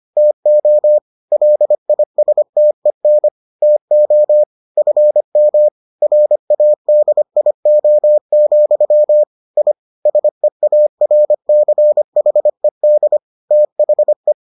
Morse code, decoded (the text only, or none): TO LISTEN TO FM RADIO, I SEARCHED THE